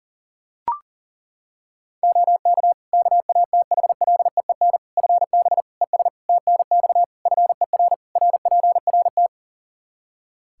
Morse code, heard as a keyed tone, about 37 wpm, 700 Hz. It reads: OK KATHLEEN FB ES TNX FER RPRT
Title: Practice QSO - Exchange 2 - Info - 40wpm